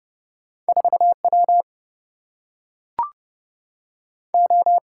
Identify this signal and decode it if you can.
Morse code, heard as a keyed tone, about 29 wpm, 700 Hz. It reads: 4W O
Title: Single-Letter Single-Number 30wpm